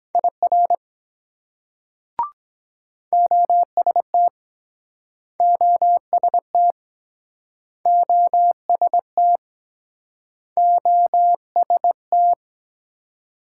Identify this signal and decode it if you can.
Morse code, heard as a keyed tone, about 23 wpm, 700 Hz. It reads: IR OST OST OST OST